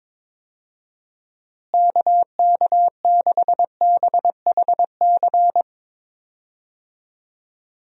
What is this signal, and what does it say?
Morse code, heard as a keyed tone, about 22 wpm, 700 Hz. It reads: KK6BHC
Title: USA Call Signs - 6 Characters - Repeated 2x - 22wpm